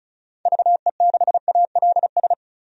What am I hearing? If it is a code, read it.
Morse code, heard as a keyed tone, about 34 wpm, 700 Hz. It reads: VE6ALS
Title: DXCC-100 Call Signs - 6 Characters - 35wpm